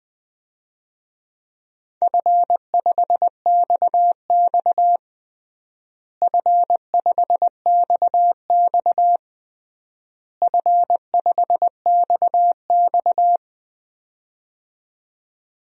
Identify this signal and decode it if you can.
Morse code, heard as a keyed tone, about 20 wpm, 700 Hz. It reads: F5XX F5XX F5XX
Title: DXCC-100 Call Signs - 4 Characters - Repeated 3x - 20wpm